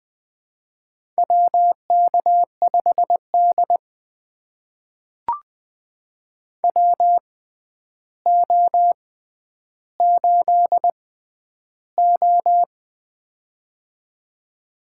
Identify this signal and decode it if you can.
Morse code, heard as a keyed tone, about 20 wpm, spaced as for 10 wpm, 700 Hz. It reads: WK5D WO8O
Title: USA Call Signs - 4 Characters - 6x Character Spacing - 20wpm